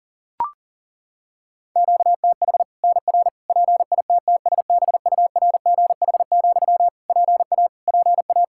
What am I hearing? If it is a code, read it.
Morse code, heard as a keyed tone, about 40 wpm, 700 Hz. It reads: QTH NR PITTSBURGH, PA PA
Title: Practice QSO - Exchange 1 - Beginning - 40wpm